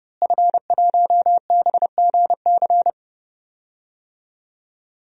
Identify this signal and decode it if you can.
Morse code, heard as a keyed tone, about 30 wpm, 700 Hz. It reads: F1BGC
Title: DXCC-100 Call Signs - 5 Characters - Repeated 3x - 30wpm